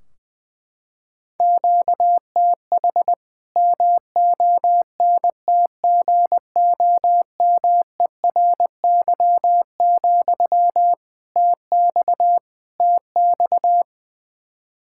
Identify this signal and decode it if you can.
Morse code, heard as a keyed tone, about 20 wpm, 700 Hz. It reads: QTH MONTGOMERY, TX TX
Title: K1USN SST QSO Practice 20wpm